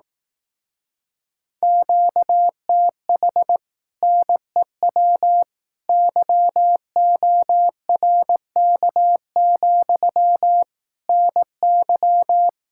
Morse code, heard as a keyed tone, about 18 wpm, 700 Hz. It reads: QTH NEW YORK, NY
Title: Speed-Racing Common QSO Phrases Encore - 15wpm